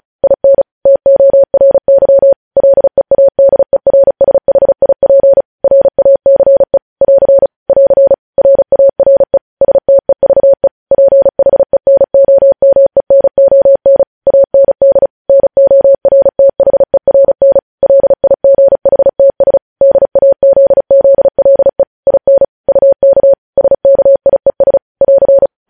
Morse code, heard as a keyed tone, about 36 wpm, 550 Hz. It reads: IN TORY LEADERSHIP RACE <AR> <AR> RARE STEVE PHENOMENON AND NORTHERN LIGHTS DAZZLE IN UK SKIES <AR>